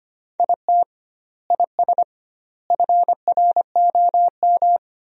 Morse code, heard as a keyed tone, about 25 wpm, 700 Hz. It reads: IT IS FROM
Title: 2x Word-Spacing - Sentences from Top 300 Words - 25wpm